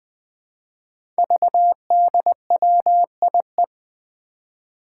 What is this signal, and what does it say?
Morse code, heard as a keyed tone, about 20 wpm, 700 Hz. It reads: VDWIE
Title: Five Challenging Letter-Number Combinations 20wpm